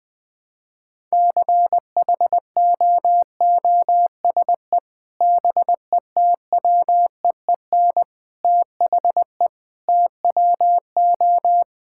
Morse code, heard as a keyed tone, about 20 wpm, 700 Hz. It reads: CHOOSE BETWEEN THE TWO
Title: Sentences with 3-Word Phrases - Spoken Beginning - 20wpm